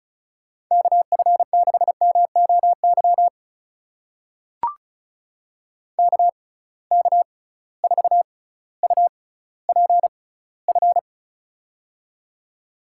Morse code, heard as a keyed tone, about 35 wpm, spaced as for 18 wpm, 700 Hz. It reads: KF6MOY KK4UPF
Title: USA Call Signs - 6 Characters - 6x Character Spacing - 35wpm